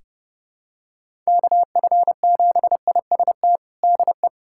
Morse code, heard as a keyed tone, about 30 wpm, 700 Hz. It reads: KF7IST DE
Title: USA Callsign DE Callsign - 6 Characters - 30wpm